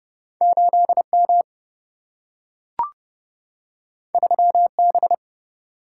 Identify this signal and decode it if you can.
Morse code, heard as a keyed tone, about 30 wpm, 700 Hz. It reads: 8M 3B
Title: Single-Letter Single-Number 30wpm